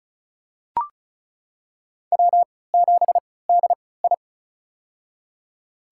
Morse code, heard as a keyed tone, about 36 wpm, spaced as for 20 wpm, 700 Hz. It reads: W7DI
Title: USA Call Signs - 4 Characters - 3x Character Spacing - 35wpm